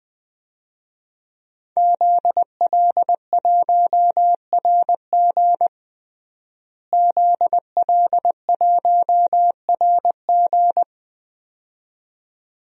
Morse code, heard as a keyed tone, about 20 wpm, 700 Hz. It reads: ZL1RG ZL1RG
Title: DXCC-100 Call Signs - 5 Characters - Repeated 2x - 20wpm